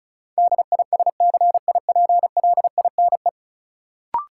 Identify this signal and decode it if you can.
Morse code, heard as a keyed tone, about 36 wpm, 700 Hz. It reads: DISCIPLINE
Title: Sentences from Top 900 Words - New Word Drill - 35wpm